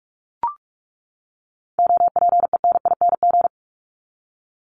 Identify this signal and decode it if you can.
Morse code, heard as a keyed tone, about 44 wpm, 700 Hz. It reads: OPENING
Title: ING Suffix - Top 100 Words with Base Words 45wpm